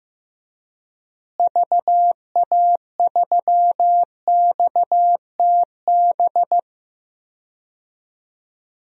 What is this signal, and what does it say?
Morse code, heard as a keyed tone, about 15 wpm, 700 Hz. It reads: VA3XTB
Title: DXCC-100 Call Signs - 6 Characters - Repeated 3x - 15wpm